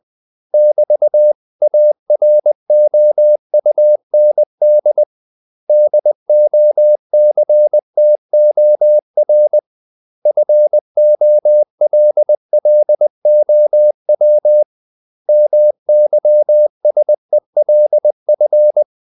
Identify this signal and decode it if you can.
Morse code, heard as a keyed tone, about 20 wpm, 600 Hz. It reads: <BT>AROUND DOCTOR FOLLOW MYSELF
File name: CW / Practice / 500MostCommonEnglishWords20